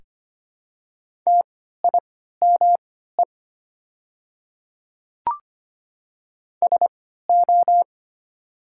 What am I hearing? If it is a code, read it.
Morse code, heard as a keyed tone, about 25 wpm, spaced as for 14 wpm, 700 Hz. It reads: TIME SO
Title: 3x Character Spacing - Top 100 Words - 25wpm